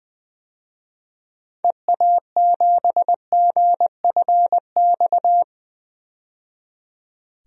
Morse code, heard as a keyed tone, about 20 wpm, 700 Hz. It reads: EA7GFX
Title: DXCC-100 Call Signs - 6 Characters - 20wpm